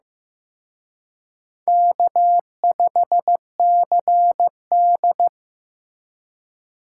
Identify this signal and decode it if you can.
Morse code, heard as a keyed tone, about 18 wpm, 700 Hz. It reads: K5CD
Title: Speed Racing - USA 1x2 Call Signs 15wpm